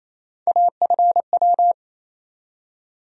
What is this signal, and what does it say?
Morse code, heard as a keyed tone, about 28 wpm, 700 Hz. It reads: AFW
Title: USA Call Sign Postfixes - 3 Letters 28wpm